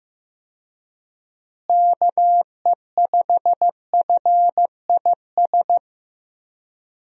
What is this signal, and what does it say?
Morse code, heard as a keyed tone, about 15 wpm, 700 Hz. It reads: KE5FIS
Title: USA Call Signs - 6 Characters - Repeated 2x - 15wpm